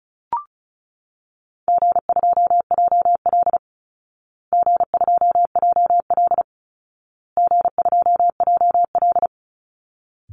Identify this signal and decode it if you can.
Morse code, heard as a keyed tone, about 34 wpm, 700 Hz. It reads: G2JL G2JL G2JL
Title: Easy Call Signs Repeated 3x - 35wpm